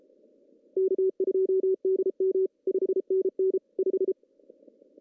Morse code, heard as a keyed tone, about 34 wpm, 380 Hz. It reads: K2DM 5NN 5
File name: CW / cqww-cw-k5tr